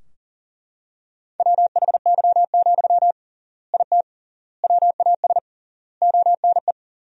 Morse code, heard as a keyed tone, about 40 wpm, 700 Hz. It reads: WHY, IT WAS ONE
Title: 3x Word-Spacing - Sentences from Top 300 Words - 40wpm